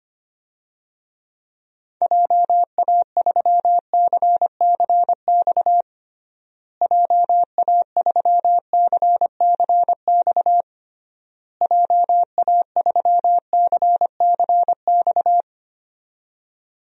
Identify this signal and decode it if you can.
Morse code, heard as a keyed tone, about 25 wpm, 700 Hz. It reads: JA3CCX JA3CCX JA3CCX
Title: DXCC-100 Call Signs - 6 Characters - Repeated 3x - 25wpm